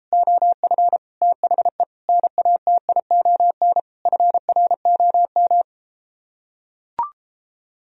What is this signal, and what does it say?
Morse code, heard as a keyed tone, about 33 wpm, 700 Hz. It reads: OF THE NATION FROM